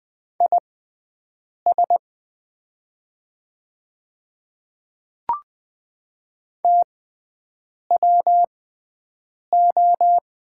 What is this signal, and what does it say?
Morse code, heard as a keyed tone, about 20 wpm, spaced as for 7 wpm, 700 Hz. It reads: IS TWO
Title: 6x Character Spacing - Top 200 Words - 20wpm